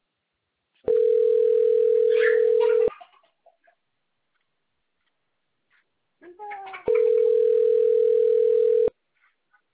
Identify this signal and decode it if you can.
Ringback tone (North America), 440 + 480 Hz.